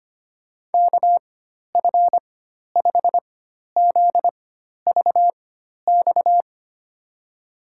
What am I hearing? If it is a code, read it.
Morse code, heard as a keyed tone, about 25 wpm, spaced as for 12 wpm, 700 Hz. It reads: KF5ZVX
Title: USA Call Signs - 6 Characters - 4x Character Spacing - 25wpm